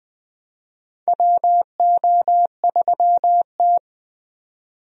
Morse code, heard as a keyed tone, about 20 wpm, 700 Hz. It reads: WO3T
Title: USA Call Signs - 4 Characters - 20wpm